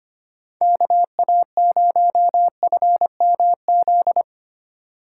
Morse code, heard as a keyed tone, about 25 wpm, 700 Hz. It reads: KA0FMZ
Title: USA Call Signs - 6 Characters - 3x Character Spacing - 25wpm